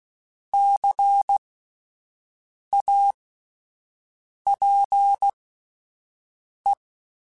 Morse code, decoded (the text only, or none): CAPE